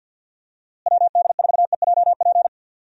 Morse code, heard as a keyed tone, about 50 wpm, 700 Hz. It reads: WD4EJP
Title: USA Call Signs - 6 Characters - Repeated 3x - 50wpm